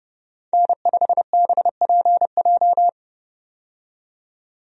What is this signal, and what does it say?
Morse code, heard as a keyed tone, about 30 wpm, 700 Hz. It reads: N5BPJ
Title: USA Call Signs - 5 Characters - 30wpm